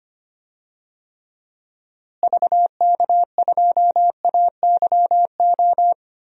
Morse code, heard as a keyed tone, about 25 wpm, 700 Hz. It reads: VK2AYO
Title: DXCC-100 Call Signs - 6 Characters - Repeated 3x - 25wpm